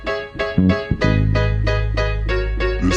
There is no code.